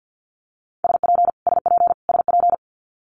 Morse code, heard as a keyed tone, about 50 wpm, 700 Hz. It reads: SP SP SP
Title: Multi-letter Phonemes Repeated 3x 50wpm